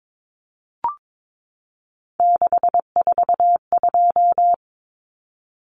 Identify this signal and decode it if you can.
Morse code, heard as a keyed tone, about 22 wpm, 700 Hz. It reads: 642
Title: Numbers - 3-Digit 22wpm